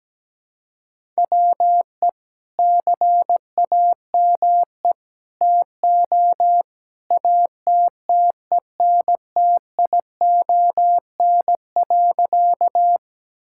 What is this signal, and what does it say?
Morse code, heard as a keyed tone, about 17 wpm, 700 Hz. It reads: WE CAME TO ATTENTION.
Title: Sentences from Top 1000 Words - Review Part 6 - 17wpm